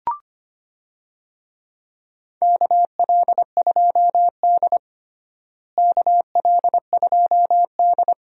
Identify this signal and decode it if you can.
Morse code, heard as a keyed tone, about 25 wpm, 700 Hz. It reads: KL2D KL2D
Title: USA Call Signs - 4 Characters - Repeated 2x - 25wpm